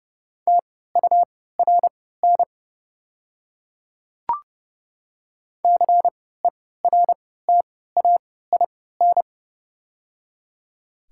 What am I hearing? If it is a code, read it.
Morse code, heard as a keyed tone, about 29 wpm, spaced as for 17 wpm, 690 Hz. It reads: TURN CERTAIN
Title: 3x Character Spacing - Top 500 Words - 30wpm